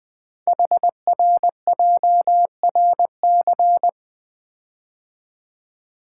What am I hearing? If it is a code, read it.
Morse code, heard as a keyed tone, about 20 wpm, 700 Hz. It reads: HRJRC
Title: Code Groups - 5 Letters - 20wpm